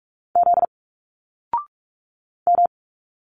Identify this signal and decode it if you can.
Morse code, heard as a keyed tone, about 46 wpm, 700 Hz. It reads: Z M
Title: Single Letters 45wpm